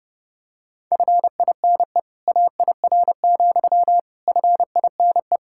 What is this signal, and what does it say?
Morse code, heard as a keyed tone, about 30 wpm, 700 Hz. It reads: FINE AIR, FINE
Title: Sentences from Top 1600 Words 30wpm